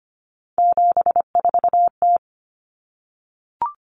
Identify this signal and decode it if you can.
Morse code, heard as a keyed tone, about 25 wpm, 700 Hz. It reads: 74T
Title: Common Cut Numbers - 3-Digit 25wpm